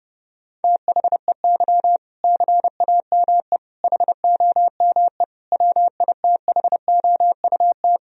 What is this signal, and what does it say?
Morse code, heard as a keyed tone, about 32 wpm, 700 Hz. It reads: THEY CAME HOME WITHOUT
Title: Sentences from Top 1000 Words - Review Part 3 - 30wpm